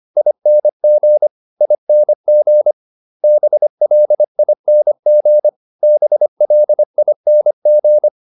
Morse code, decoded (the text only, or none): ING ING BLING BLING